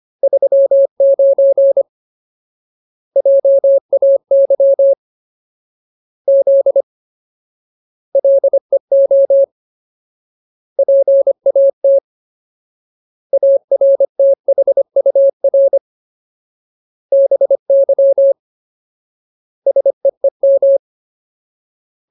Morse code, heard as a keyed tone, about 25 wpm, 550 Hz. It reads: 39 JAY Z LEO PAT ARTHUR BY SEEM
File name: Ultimate Word List / 25WPM Ultimate